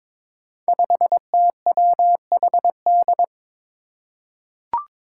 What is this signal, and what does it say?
Morse code, heard as a keyed tone, about 21 wpm, 700 Hz. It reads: 5TWHD
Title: Five Challenging Letter-Number Combinations 22wpm